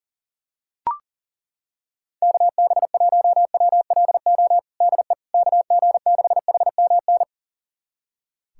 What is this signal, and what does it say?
Morse code, heard as a keyed tone, about 37 wpm, 700 Hz. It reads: KB1WLO DE KG6HMN
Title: USA Callsign DE Callsign 40wpm